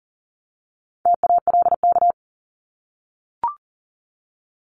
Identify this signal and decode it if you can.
Morse code, heard as a keyed tone, about 40 wpm, 700 Hz. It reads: TALK
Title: ING Suffix - Top 100 Words with Base Words 40wpm